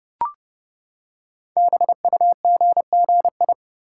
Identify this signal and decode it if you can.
Morse code, heard as a keyed tone, about 30 wpm, 700 Hz. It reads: BUGGI
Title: Code Groups - 5 Letter:Numbers - 30wpm